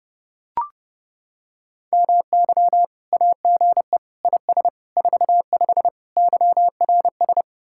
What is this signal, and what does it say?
Morse code, heard as a keyed tone, about 30 wpm, 700 Hz. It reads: MY AGE IS 45 YRS